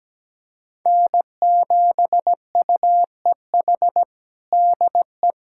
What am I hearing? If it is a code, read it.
Morse code, heard as a keyed tone, about 17 wpm, 700 Hz. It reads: N7UEH DE
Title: USA Callsign DE Callsign - 5 Characters - 17wpm